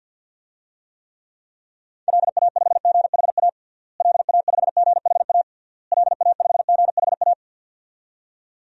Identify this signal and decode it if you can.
Morse code, heard as a keyed tone, about 52 wpm, 700 Hz. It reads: LA5GHA LA5GHA LA5GHA
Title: DXCC-100 Call Signs - 6 Characters - Repeated 3x - 50wpm